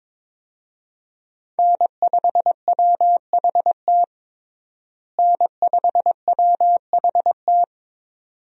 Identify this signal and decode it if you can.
Morse code, heard as a keyed tone, about 22 wpm, 700 Hz. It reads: N5WHT N5WHT
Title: USA Call Signs - 5 Characters - Repeated 2x - 22wpm